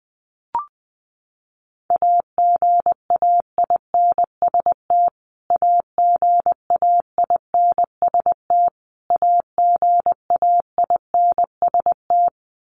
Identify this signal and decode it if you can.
Morse code, heard as a keyed tone, about 20 wpm, 700 Hz. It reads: AGAINST AGAINST AGAINST
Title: Top 500 Words Repeated 20wpm